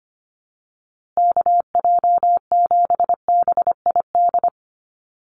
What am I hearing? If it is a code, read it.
Morse code, heard as a keyed tone, about 25 wpm, 700 Hz. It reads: KJ7BID
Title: USA Call Signs Repeated 3x 25wpm